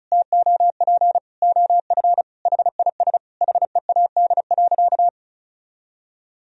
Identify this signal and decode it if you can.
Morse code, heard as a keyed tone, about 36 wpm, 700 Hz. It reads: TOP OF HIS HEAD.
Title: Sentences from Top 1000 Words - Review Part 2 - 35wpm